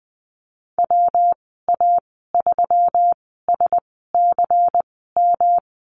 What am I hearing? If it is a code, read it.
Morse code, heard as a keyed tone, about 20 wpm, spaced as for 14 wpm, 700 Hz. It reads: WA3SCM
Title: 128 Lesson 21g - Number 6 - Call Signs 20wpm